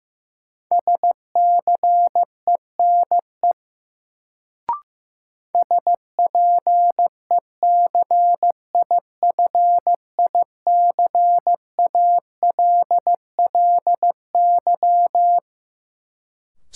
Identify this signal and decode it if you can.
Morse code, heard as a keyed tone, about 15 wpm, 700 Hz. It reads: SCENE SPECIFICALLY
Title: Sentences from Top 1100 Words - New Word Drill - 15wpm